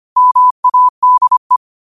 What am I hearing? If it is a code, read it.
Morse code, heard as a keyed tone, about 25 wpm, 1000 Hz. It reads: MADE
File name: qotdfiles / 25wpm